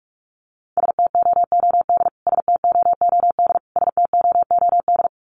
Morse code, heard as a keyed tone, about 44 wpm, 700 Hz. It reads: STOOD STOOD STOOD
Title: Top 500 Words Repeated 45wpm